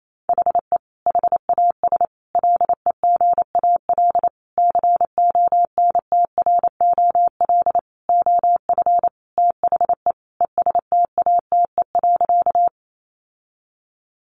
Morse code, heard as a keyed tone, about 28 wpm, 700 Hz. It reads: HE HAS LEGAL CONTROL OF THE ESTATE.
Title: Sentences from Top 1200 Words 28wpm